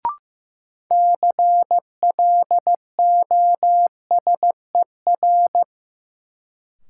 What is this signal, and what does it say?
Morse code, heard as a keyed tone, about 15 wpm, 700 Hz. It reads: CLOSER
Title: Sets of 5 Words Encore - New Word Drill - 15wpm